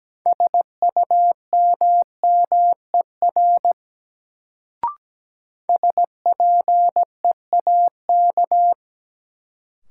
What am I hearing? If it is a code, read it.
Morse code, heard as a keyed tone, about 17 wpm, 700 Hz. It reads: SUMMER SPEAK
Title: Sentences from Top 1100 Words - New Word Drill Alt - 17wpm